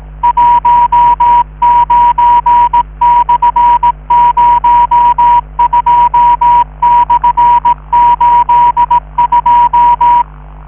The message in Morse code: 19/02/82